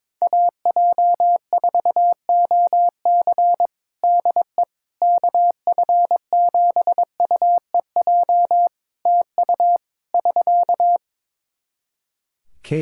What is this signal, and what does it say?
Morse code, heard as a keyed tone, about 22 wpm, 700 Hz. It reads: AJ4OC DE KF7UEJ TU <SK>
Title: Practice QSO - Exchange 4 - Ending - 22wpm